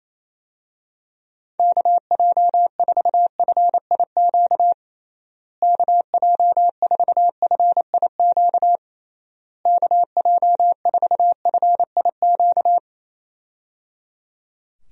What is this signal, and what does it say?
Morse code, heard as a keyed tone, about 28 wpm, 700 Hz. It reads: KJ4FIQ KJ4FIQ KJ4FIQ
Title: USA Call Signs - 6 Characters - Repeated 3x - 28wpm